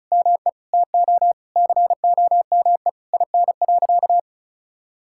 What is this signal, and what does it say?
Morse code, heard as a keyed tone, about 36 wpm, 700 Hz. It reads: ME TO COME IN.
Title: Sentences from Top 1000 Words - Review Part 5 - 35wpm